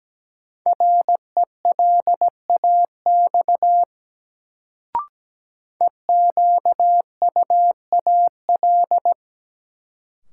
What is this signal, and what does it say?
Morse code, heard as a keyed tone, about 17 wpm, 700 Hz. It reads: RELAX EQUAL
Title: Sentences from Top 1400 Words - New Word Drill - 17wpm